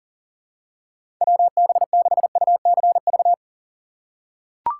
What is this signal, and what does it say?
Morse code, heard as a keyed tone, about 37 wpm, 700 Hz. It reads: WB6UCV
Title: USA Call Signs - 6 Characters - 4x Character Spacing - 40wpm